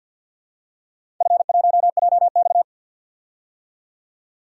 Morse code, heard as a keyed tone, about 53 wpm, 700 Hz. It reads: F1JX
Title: DXCC-100 Call Signs - 4 Characters - 50wpm